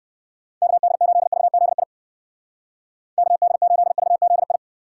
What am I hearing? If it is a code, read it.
Morse code, heard as a keyed tone, about 56 wpm, 700 Hz. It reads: KD8VZI KD8VZI